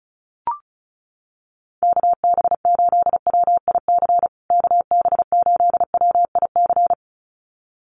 Morse code, heard as a keyed tone, about 34 wpm, 700 Hz. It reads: KB8WIC KB8WIC
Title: USA Call Signs Repeated 2x 35wpm